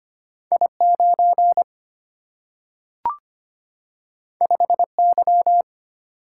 Morse code, heard as a keyed tone, about 25 wpm, 700 Hz. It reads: I9 5Y
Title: Single-Letter Single-Number 25wpm